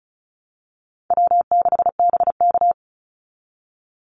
Morse code, heard as a keyed tone, about 34 wpm, 700 Hz. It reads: W6BK
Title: Call Signs Repeated 3x - 35wpm